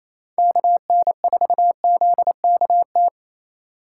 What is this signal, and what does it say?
Morse code, heard as a keyed tone, about 28 wpm, 700 Hz. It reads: KN4ZKT
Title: K1USN SST QSO Practice - 28wpm